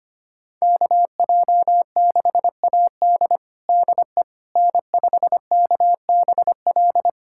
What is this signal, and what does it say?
Morse code, heard as a keyed tone, about 25 wpm, 700 Hz. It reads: KJ6AD DE N5KBL
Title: USA Callsign DE Callsign - 5 Characters - 25wpm